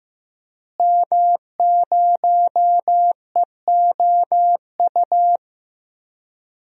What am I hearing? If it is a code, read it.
Morse code, heard as a keyed tone, about 15 wpm, 700 Hz. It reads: M0EOU
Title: DXCC-100 Call Signs - 5 Characters - Repeated 3x - 15wpm